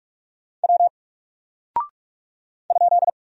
Morse code, heard as a keyed tone, about 46 wpm, 700 Hz. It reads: W ?
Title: Single Character (Letter:Number:Punctuation:Prosign) 45wpm